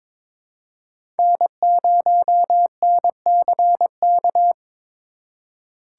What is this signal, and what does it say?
Morse code, heard as a keyed tone, about 22 wpm, 700 Hz. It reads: N0NCK
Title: USA Call Signs - 5 Characters - 4x Character Spacing - 22wpm